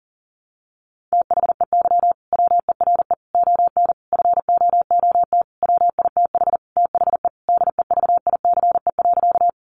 Morse code, heard as a keyed tone, about 40 wpm, 700 Hz. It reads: THEY WERE ON FOOT WITH THE DEVICE.
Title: Sentences from Top 1500 Words 40wpm